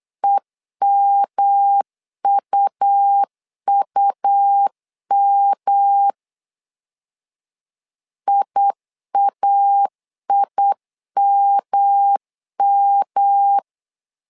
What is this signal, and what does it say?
Morse code, heard as a keyed tone, about 8 wpm, 800 Hz. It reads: EMUUM IAIMM
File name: LEZIONI / 3 MWRO / 3 M 8wpm 2300